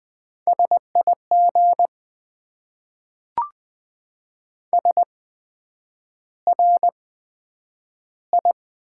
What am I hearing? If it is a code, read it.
Morse code, heard as a keyed tone, about 20 wpm, spaced as for 9 wpm, 700 Hz. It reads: SIG SRI
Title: Top 90 QSO Elements - 8x Character Spacing - 20wpm